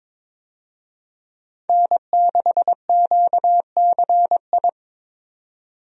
Morse code, heard as a keyed tone, about 21 wpm, 700 Hz. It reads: N6QCI
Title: USA Call Signs - 5 Characters - 22wpm